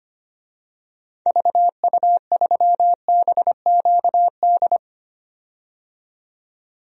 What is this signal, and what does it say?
Morse code, heard as a keyed tone, about 25 wpm, 700 Hz. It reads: VU3BQD
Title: DXCC-100 Call Signs - 6 Characters - 25wpm